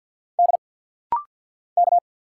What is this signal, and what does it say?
Morse code, heard as a keyed tone, about 50 wpm, 700 Hz. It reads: D K
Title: Single Letters - Rapid-Fire 50wpm